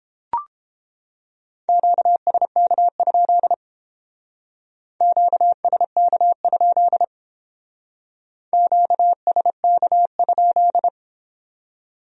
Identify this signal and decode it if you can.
Morse code, heard as a keyed tone, about 30 wpm, 700 Hz. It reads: QSK? QSK? QSK?